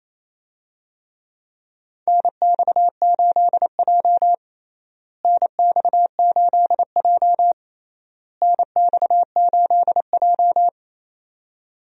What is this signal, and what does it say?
Morse code, heard as a keyed tone, about 28 wpm, 700 Hz. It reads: NX8J NX8J NX8J
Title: USA Call Signs - 4 Characters - Repeated 3x - 28wpm